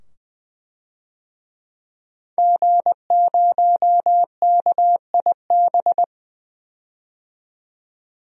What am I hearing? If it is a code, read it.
Morse code, heard as a keyed tone, about 20 wpm, 700 Hz. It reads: G0KIB
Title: DXCC-100 Call Signs - 5 Characters - 20wpm